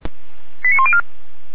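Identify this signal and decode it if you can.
Five-tone selective call (ZVEI), heard as quick runs of five tones, tones of about 70 ms. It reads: 89104
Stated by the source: ZVEI five-tone call 89319 received on SDR, logged by feuerwehr-orlamuende fire brigade.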